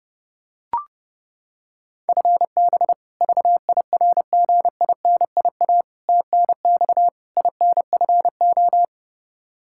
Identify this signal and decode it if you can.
Morse code, heard as a keyed tone, about 30 wpm, 700 Hz. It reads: FB VIRGINIA TNX INFO
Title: K1USN SST QSO Practice 30wpm